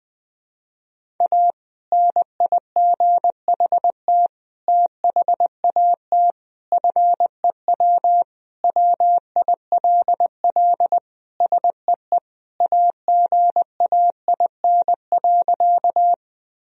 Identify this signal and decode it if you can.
Morse code, heard as a keyed tone, about 20 wpm, 700 Hz. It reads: A NIGHT THAT FEW WILL SEE AGAIN.
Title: Sentences from Top 1000 Words - Review Part 1 - 20wpm